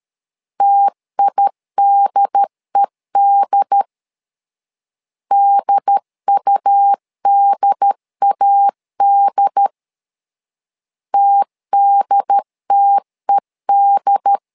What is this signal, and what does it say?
Morse code, heard as a keyed tone, about 13 wpm, 800 Hz. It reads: TIDED DUDAD TDTED
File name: LEZIONI / 2 UHDN / 2 UHD 12wpm 1540